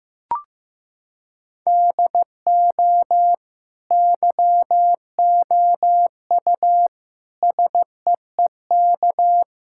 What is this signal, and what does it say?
Morse code, heard as a keyed tone, about 15 wpm, 700 Hz. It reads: DO YOU SEEK
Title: Sentences from Top 1600 Words 15wpm